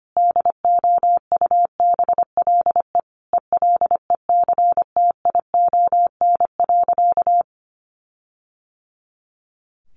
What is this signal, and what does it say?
Morse code, heard as a keyed tone, about 25 wpm, 700 Hz. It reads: DOUBLE ELECTION.
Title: Sentences from Top 1500 Words 25wpm